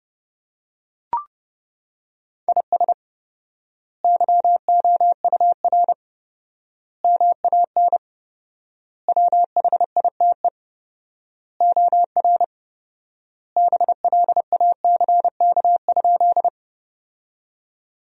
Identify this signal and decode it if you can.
Morse code, heard as a keyed tone, about 30 wpm, 700 Hz. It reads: IS YOUR MAN WHITE OR BLACK?
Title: 4x Word-Spacing - Sentences from Top 400 Words - 30wpm